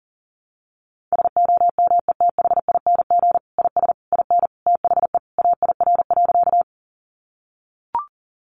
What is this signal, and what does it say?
Morse code, heard as a keyed tone, about 40 wpm, 700 Hz. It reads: SOMETHING IS IN THE AIR.
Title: Sentences from Top 600 Words 40wpm